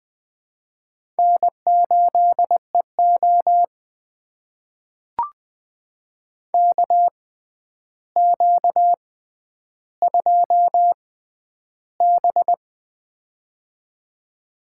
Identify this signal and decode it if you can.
Morse code, heard as a keyed tone, about 20 wpm, spaced as for 10 wpm, 700 Hz. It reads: N8EO KQ2B
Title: USA Call Signs - 4 Characters - 6x Character Spacing - 20wpm